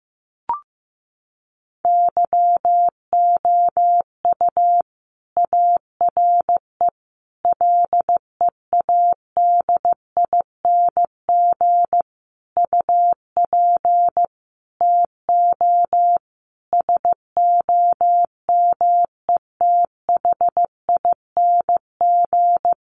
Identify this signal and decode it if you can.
Morse code, heard as a keyed tone, about 15 wpm, 700 Hz. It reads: YOU ARE LEADING UP TO SOMETHING
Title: Sentences from Top 1100 Words 15wpm